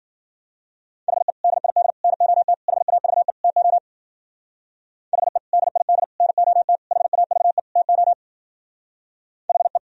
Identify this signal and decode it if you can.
Morse code, heard as a keyed tone, about 50 wpm, 700 Hz. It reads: HE DID NOT HAVE TO HE DID NOT HAVE TO HE